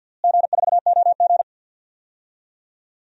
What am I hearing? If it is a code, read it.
Morse code, heard as a keyed tone, about 46 wpm, 700 Hz. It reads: G4OG
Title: DXCC-100 Call Signs - 4 Characters - Repeated 3x - 50wpm